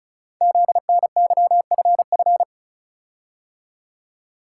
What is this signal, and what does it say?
Morse code, heard as a keyed tone, about 36 wpm, 700 Hz. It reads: ZNYFF
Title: Code Groups - 5 Letters - 35wpm